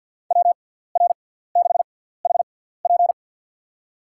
Morse code, heard as a keyed tone, about 51 wpm, spaced as for 17 wpm, 700 Hz. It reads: WR6HP
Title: USA Call Signs - 5 Characters - 6x Character Spacing - 50wpm